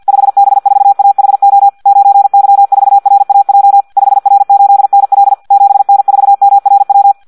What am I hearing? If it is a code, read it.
Morse code, heard as a keyed tone, about 50 wpm, 800 Hz. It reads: 5B4AHO 9Y4RAJ 5R8NL 7N4MRW
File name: Audio practica CW / qrq-tx-50 2